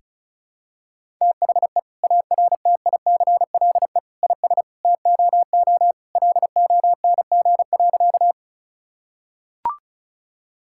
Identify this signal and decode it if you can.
Morse code, heard as a keyed tone, about 36 wpm, 700 Hz. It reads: THE ARTICLE IS TOO LONG.
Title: Sentences from Top 1000 Words - Review Part 5 - 35wpm